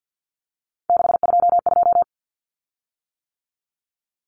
Morse code, heard as a keyed tone, about 48 wpm, 700 Hz. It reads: 622
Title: Common Cut Numbers - 3-Digit 50wpm